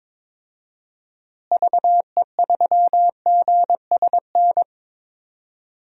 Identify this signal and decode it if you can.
Morse code, heard as a keyed tone, about 22 wpm, 700 Hz. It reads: VE3GSN
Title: DXCC-100 Call Signs - 6 Characters - Repeated 2x - 22wpm